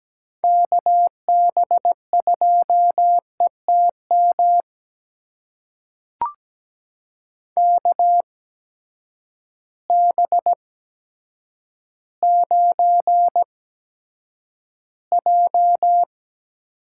Morse code, heard as a keyed tone, about 17 wpm, spaced as for 8 wpm, 700 Hz. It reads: KB2ETM KB9J
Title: USA Call Signs - 6 Characters - 8x Character Spacing - 17wpm